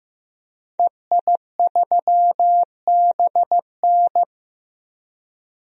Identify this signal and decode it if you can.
Morse code, heard as a keyed tone, about 15 wpm, 700 Hz. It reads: EI3BN
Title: DXCC-100 Call Signs - 5 Characters - Repeated 3x - 15wpm